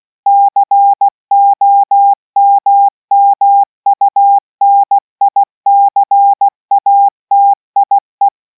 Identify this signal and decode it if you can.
Morse code, heard as a keyed tone, about 16 wpm, 800 Hz. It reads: COMMUNICATIE